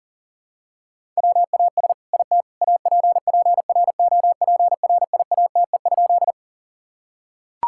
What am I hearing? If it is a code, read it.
Morse code, heard as a keyed tone, about 40 wpm, 700 Hz. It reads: WAS IT APPROPRIATE?
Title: Sentences from Top 1000 Words - Review Part 5 - 40wpm